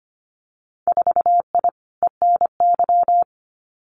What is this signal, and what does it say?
Morse code, heard as a keyed tone, about 25 wpm, 700 Hz. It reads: 4I ENY
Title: Winter Field Day Exchanges Encore 25wpm